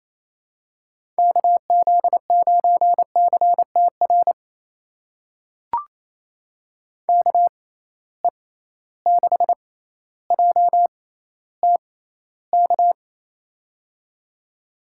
Morse code, heard as a keyed tone, about 28 wpm, spaced as for 14 wpm, 700 Hz. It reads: KZ9CTR KE6JTK
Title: USA Call Signs - 6x Character Spacing - 28wpm